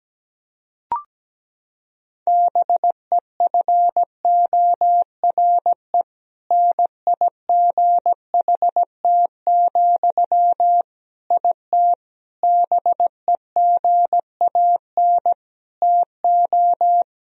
Morse code, heard as a keyed tone, about 17 wpm, 700 Hz. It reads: BEFORE NIGHT, IT BEGAN TO
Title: Sentences from Top 1000 Words - Review Part 1 - 17wpm